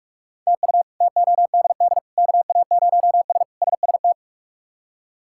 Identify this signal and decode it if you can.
Morse code, heard as a keyed tone, about 44 wpm, 700 Hz. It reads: TU TODD KA0S SST
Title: K1USN SST QSO Practice - 45wpm